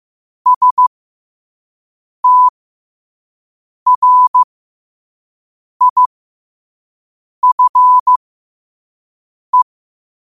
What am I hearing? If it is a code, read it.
Morse code, heard as a keyed tone, about 15 wpm, spaced as for 5 wpm, 1000 Hz. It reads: STRIFE
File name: qotdfiles / 5wpm